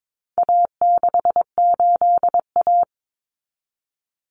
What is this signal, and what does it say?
Morse code, heard as a keyed tone, about 22 wpm, 700 Hz. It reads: A68A
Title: Common Cut Numbers - 4-Digit 22wpm